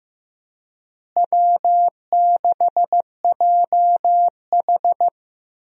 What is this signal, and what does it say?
Morse code, heard as a keyed tone, about 16 wpm, 700 Hz. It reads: W6JH
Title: USA Call Signs - 4 Characters - Repeated 3x - 15wpm